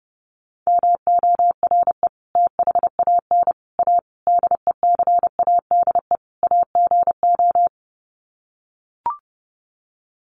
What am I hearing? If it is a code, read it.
Morse code, heard as a keyed tone, about 28 wpm, 700 Hz. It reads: MORE THAN A DECADE AGO
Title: Sets of 5 Words Encore2 30wpm